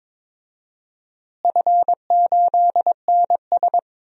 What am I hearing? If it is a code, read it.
Morse code, heard as a keyed tone, about 21 wpm, 700 Hz. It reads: F8NS
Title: DXCC-100 Call Signs - 4 Characters - Repeated 2x - 22wpm